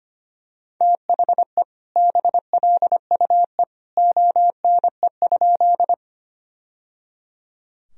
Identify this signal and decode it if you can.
Morse code, heard as a keyed tone, about 25 wpm, 700 Hz. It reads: THE BLUE ONE?
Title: Sentences from Top 1000 Words - Review Part 3 - 25wpm